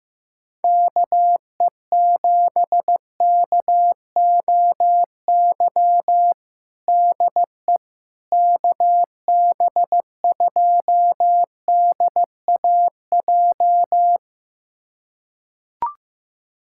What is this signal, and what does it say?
Morse code, heard as a keyed tone, about 15 wpm, 700 Hz. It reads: KE7KOY DE KB2DAJ
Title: USA Callsign DE Callsign - 6 Characters - 15wpm